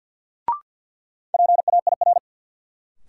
Morse code, heard as a keyed tone, about 45 wpm, 700 Hz. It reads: PAIR
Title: Sentences from Top 1200 Words - New Word Drill Alt - 50wpm